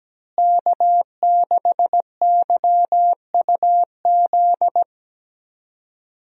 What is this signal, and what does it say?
Morse code, heard as a keyed tone, about 17 wpm, 700 Hz. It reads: K6YUZ
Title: USA Call Signs - 5 Characters - Repeated 2x - 17wpm